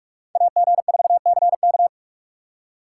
Morse code, heard as a keyed tone, about 44 wpm, 700 Hz. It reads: AG4CK